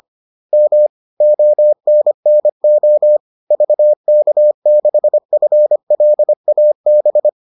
Morse code, heard as a keyed tone, about 25 wpm, 600 Hz. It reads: M ONNO VK6FLAB